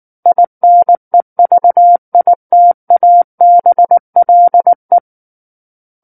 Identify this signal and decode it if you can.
Morse code, heard as a keyed tone, about 19 wpm, 700 Hz. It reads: INEVITABLE
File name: Book One / 10-Chapter 10 - 19 WPM